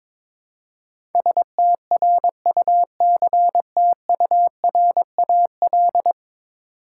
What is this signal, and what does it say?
Morse code, heard as a keyed tone, about 22 wpm, 700 Hz. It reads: STRUCTURAL
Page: morse-code-ninja-practice-22wpm